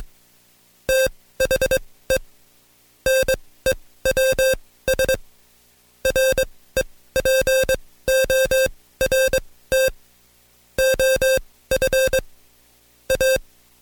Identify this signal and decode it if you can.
Morse code, heard as a keyed tone, about 21 wpm, spaced as for 16 wpm, 530 Hz. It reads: THE NEWS REPORT OF A
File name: Code Quick / AudioDownld2 / Seg134-Stepper7 15WPM